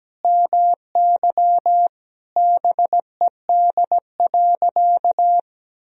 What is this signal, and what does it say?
Morse code, heard as a keyed tone, about 17 wpm, 700 Hz. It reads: MY BED.
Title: Sentences from Top 1000 Words - Review Part 4 - 17wpm